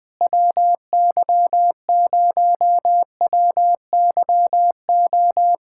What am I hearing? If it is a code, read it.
Morse code, heard as a keyed tone, about 20 wpm, 700 Hz. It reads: WY0WYO
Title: SOTA QSO Practice 20wpm